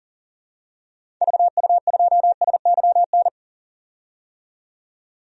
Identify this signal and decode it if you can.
Morse code, heard as a keyed tone, about 38 wpm, 700 Hz. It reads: VU2SYN
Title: DXCC-100 Call Signs - 6 Characters - 40wpm